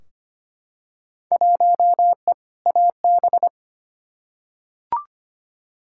Morse code, heard as a keyed tone, about 25 wpm, 700 Hz. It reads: 1E AB
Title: Field Day Exchanges - 25wpm